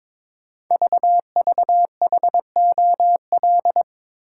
Morse code, heard as a keyed tone, about 22 wpm, 690 Hz. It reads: VVHOL
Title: Code Groups - 5 Letters - 22wpm